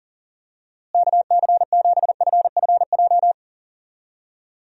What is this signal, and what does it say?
Morse code, heard as a keyed tone, about 40 wpm, 700 Hz. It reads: KC7FFJ
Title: USA Call Signs - 6 Characters - Repeated 2x - 40wpm